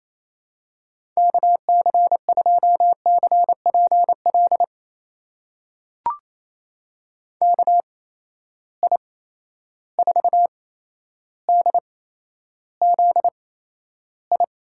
Morse code, heard as a keyed tone, about 28 wpm, spaced as for 12 wpm, 700 Hz. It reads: KC2CPL KI4DZI
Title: USA Call Signs - 8x Character Spacing - 28wpm